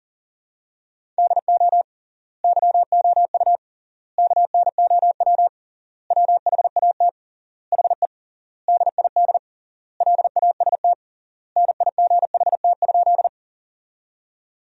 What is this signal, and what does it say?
Morse code, heard as a keyed tone, about 40 wpm, 700 Hz. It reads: DO YOU KNOW WHAT HE DID LAST NIGHT?
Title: 3x Word-Spacing - Sentences from Top 300 Words - 40wpm